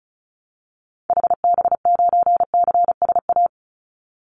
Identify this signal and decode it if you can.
Morse code, heard as a keyed tone, about 34 wpm, 700 Hz. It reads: HB9CSA
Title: Call Signs - 35wpm